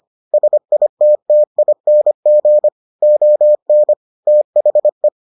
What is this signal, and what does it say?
Morse code, heard as a keyed tone, about 25 wpm, 600 Hz. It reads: SITTING ON THE